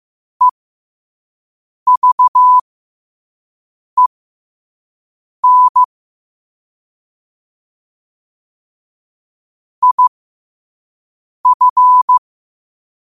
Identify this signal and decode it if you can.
Morse code, heard as a keyed tone, about 15 wpm, spaced as for 5 wpm, 1000 Hz. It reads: EVEN IF
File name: qotdfiles / 5wpm